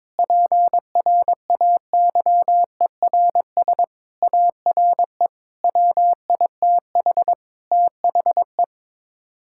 Morse code, heard as a keyed tone, about 22 wpm, 700 Hz. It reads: PRAYERS ARE WITH THE
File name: practice / 22wpm / Sets of 7 Words - 22wpm